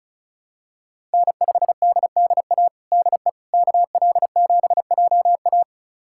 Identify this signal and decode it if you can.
Morse code, heard as a keyed tone, about 36 wpm, 700 Hz. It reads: N5DDA DE KL7JA
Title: USA Callsign DE Callsign - 5 Characters - 35wpm